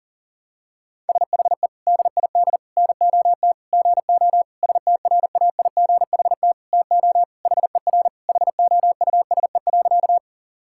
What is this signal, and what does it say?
Morse code, heard as a keyed tone, about 40 wpm, 700 Hz. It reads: SHE DID NOT GO STRAIGHT TO HER HOUSE.
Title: Sentences from Top 1000 Words - Review Part 6 - 40wpm